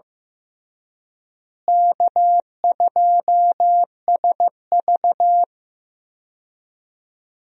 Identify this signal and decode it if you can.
Morse code, heard as a keyed tone, about 15 wpm, 700 Hz. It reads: K2SV
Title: USA Call Signs - 4 Characters - Repeated 3x - 15wpm